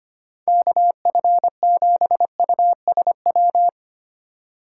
Morse code, heard as a keyed tone, about 25 wpm, 700 Hz. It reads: KF7USW
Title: USA Call Signs - 8x Character Spacing - 25wpm